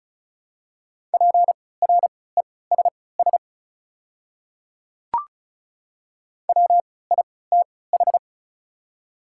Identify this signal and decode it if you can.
Morse code, heard as a keyed tone, about 36 wpm, spaced as for 20 wpm, 700 Hz. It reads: PRESS WITH
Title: 3x Character Spacing - Top 500 Words - 35wpm